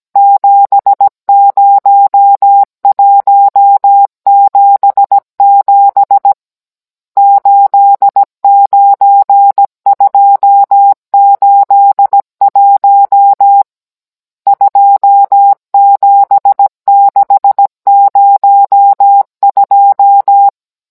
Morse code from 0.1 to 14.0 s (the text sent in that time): 70177 89281